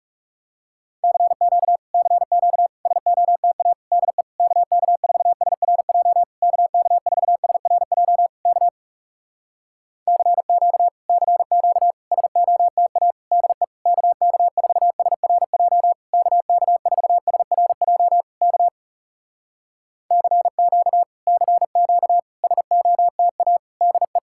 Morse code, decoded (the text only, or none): CQ CQ SOTA DE KK4SRJ KK4SRJ K CQ CQ SOTA DE KK4SRJ KK4SRJ K CQ CQ SOTA DE